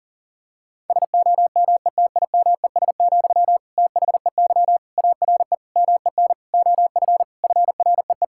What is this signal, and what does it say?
Morse code, heard as a keyed tone, about 40 wpm, 700 Hz. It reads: SOMETIMES, THEY ARE MEN OF FREE
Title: Sentences from Top 1000 Words - Review Part 5 - 40wpm